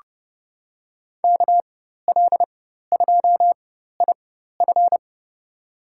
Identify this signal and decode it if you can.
Morse code, heard as a keyed tone, about 30 wpm, spaced as for 14 wpm, 700 Hz. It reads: KL2IF
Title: USA Call Signs - 5 Characters - 4x Character Spacing - 30wpm